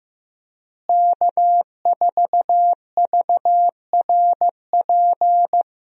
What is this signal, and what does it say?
Morse code, heard as a keyed tone, about 15 wpm, 700 Hz. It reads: K4VRP
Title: USA Call Signs - 5 Characters - Repeated 2x - 15wpm